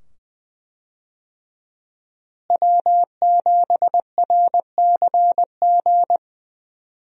Morse code, heard as a keyed tone, about 20 wpm, 700 Hz. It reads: W7RCG
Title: USA Call Signs - 5 Characters - Repeated 2x - 20wpm